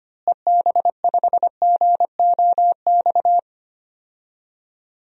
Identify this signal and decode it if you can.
Morse code, heard as a keyed tone, about 25 wpm, 700 Hz. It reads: EB5GOX
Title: DXCC-100 Call Signs - 6 Characters - Repeated 3x - 25wpm